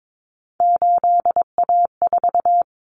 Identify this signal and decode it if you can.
Morse code, heard as a keyed tone, about 22 wpm, 700 Hz. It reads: 8A4
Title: Common Cut Numbers - 3-Digit 22wpm